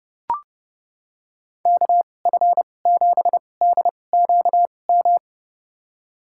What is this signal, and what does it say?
Morse code, heard as a keyed tone, about 29 wpm, spaced as for 22 wpm, 700 Hz. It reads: KF7DQM
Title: USA Call Signs - 2x Character Spacing - 30wpm